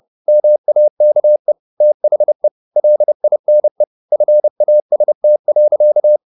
Morse code, heard as a keyed tone, about 30 wpm, 600 Hz. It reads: MAKE THE LINE FAST.
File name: the old man and the sea / the old man and the sea 1 30wpm